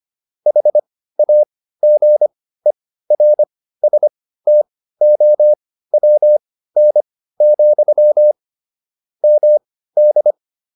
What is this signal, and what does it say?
Morse code, heard as a keyed tone, about 25 wpm, spaced as for 15 wpm, 600 Hz. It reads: HAGERSTOWN, MD
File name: QSO307 15